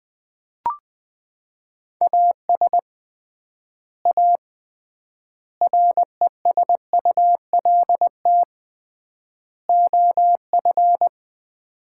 Morse code, heard as a keyed tone, about 21 wpm, 700 Hz. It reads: AS A RESULT OF
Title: 3x Word-Spacing - Sets of 4 Words - 20wpm